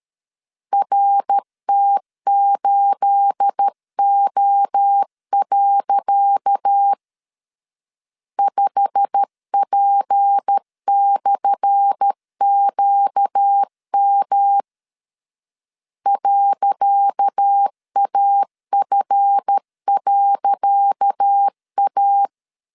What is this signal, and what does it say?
Morse code, heard as a keyed tone, about 13 wpm, 800 Hz. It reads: RT8O. 5P/QM .AF.A
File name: LEZIONI / 10 SIGNS / 10C DOT 12wpm 2451